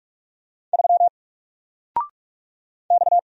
Morse code, heard as a keyed tone, about 45 wpm, 700 Hz. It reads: 3 X
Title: Single Letter-Number 45wpm